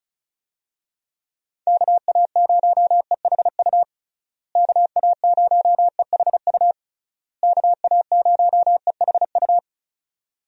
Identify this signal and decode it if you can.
Morse code, heard as a keyed tone, about 36 wpm, 700 Hz. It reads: KA0EHU KA0EHU KA0EHU
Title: USA Call Signs - 6 Characters - Repeated 3x - 35wpm